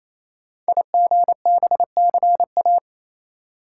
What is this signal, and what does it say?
Morse code, heard as a keyed tone, about 27 wpm, 700 Hz. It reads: IGBCA
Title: Code Groups - 5 Letters - 28wpm